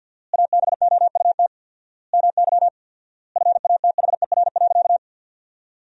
Morse code, heard as a keyed tone, about 51 wpm, 700 Hz. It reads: ABOUT MY FATHER.
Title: 4x Word-Spacing - Sentences from Top 400 Words - 50wpm